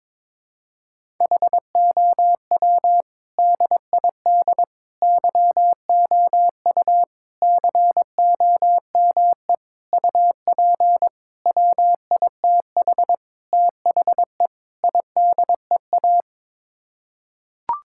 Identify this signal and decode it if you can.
Morse code, heard as a keyed tone, about 22 wpm, 700 Hz. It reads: HOW DID YOU COME UP WITH THE IDEA
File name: practice / 22wpm / Sets of 8 Words - 22wpm